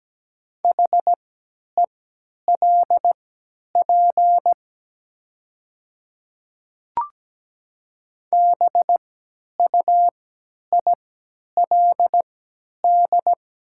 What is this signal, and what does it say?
Morse code, heard as a keyed tone, about 17 wpm, spaced as for 10 wpm, 700 Hz. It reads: HELP BUILD
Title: 3x Character Spacing - Top 200 Words - 17wpm